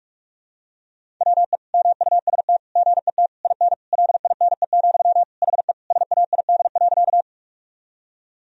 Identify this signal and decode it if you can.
Morse code, heard as a keyed tone, about 41 wpm, 700 Hz. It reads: WE MUST GET IN LINE, HE SAID.
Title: Sentences from Top 1000 Words - Review Part 1 - 45wpm